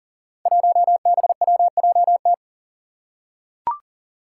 Morse code, heard as a keyed tone, about 40 wpm, 700 Hz. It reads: 1BWJT
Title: Five Challenging Letter-Number Combinations 40wpm